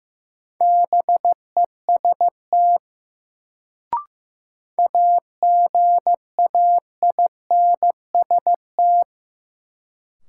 Morse code, heard as a keyed tone, about 15 wpm, 700 Hz. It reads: BEST AGAINST
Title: Sets of 2 Words - New Word Drill - 15wpm